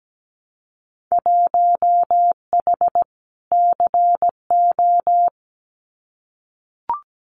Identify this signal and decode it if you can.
Morse code, heard as a keyed tone, about 17 wpm, 700 Hz. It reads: 1H CO
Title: Winter Field Day Exchanges 17wpm